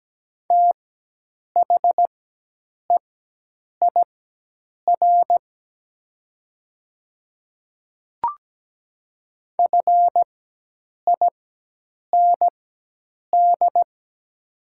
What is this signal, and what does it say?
Morse code, heard as a keyed tone, about 17 wpm, spaced as for 8 wpm, 700 Hz. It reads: THEIR FIND
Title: 4x Character Spacing - Top 100 Words - 17wpm